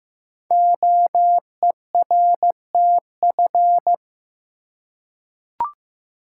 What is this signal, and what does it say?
Morse code, heard as a keyed tone, about 15 wpm, 700 Hz. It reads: OERTF
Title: Code Groups - 5 Letters - 15wpm